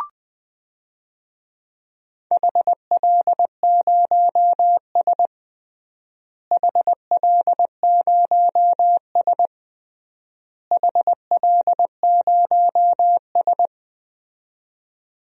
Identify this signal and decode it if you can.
Morse code, heard as a keyed tone, about 20 wpm, 700 Hz. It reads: HL0S HL0S HL0S
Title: DXCC-100 Call Signs - 4 Characters - Repeated 3x - 20wpm